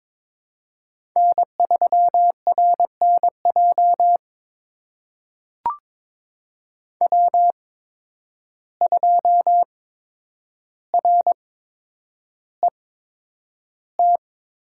Morse code, heard as a keyed tone, about 22 wpm, spaced as for 9 wpm, 700 Hz. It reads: N3RNJ W2RET
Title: USA Call Signs - 5 Characters - 8x Character Spacing - 22wpm